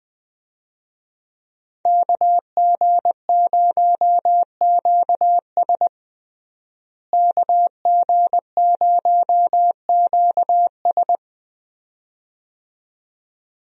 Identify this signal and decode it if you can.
Morse code, heard as a keyed tone, about 21 wpm, 700 Hz. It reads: KG0QS KG0QS
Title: USA Call Signs - 5 Characters - Repeated 2x - 20wpm